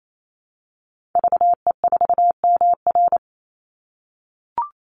Morse code, heard as a keyed tone, about 28 wpm, 700 Hz. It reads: VE4MR
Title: Call Signs - 28wpm